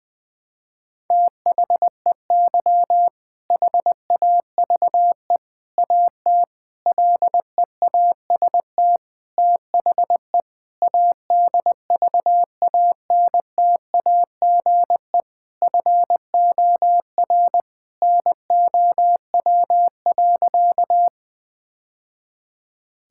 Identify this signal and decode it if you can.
Morse code, heard as a keyed tone, about 20 wpm, 700 Hz. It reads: THEY HAVE AT LEAST THE ADVANTAGE FOR NOW.
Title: Sentences from Top 1000 Words - Review Part 6 - 20wpm